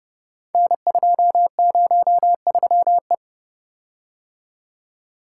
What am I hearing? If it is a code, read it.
Morse code, heard as a keyed tone, about 30 wpm, 700 Hz. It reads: N203E
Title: Code Groups - 5 Letter:Numbers - 30wpm